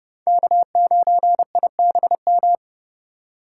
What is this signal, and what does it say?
Morse code, heard as a keyed tone, about 30 wpm, 700 Hz. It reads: K9IBM
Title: K1USN SST QSO Practice - 30wpm